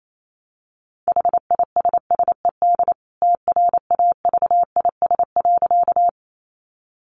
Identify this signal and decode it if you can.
Morse code, heard as a keyed tone, about 28 wpm, 700 Hz. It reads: HISSED TRAVIS.
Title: A Sound Of Thunder 28wpm